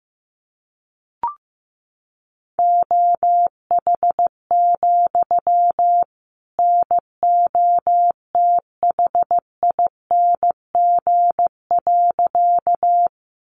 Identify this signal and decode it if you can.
Morse code, heard as a keyed tone, about 15 wpm, 700 Hz. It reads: OH, NOTHING.
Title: Sentences from Top 500 Words 15wpm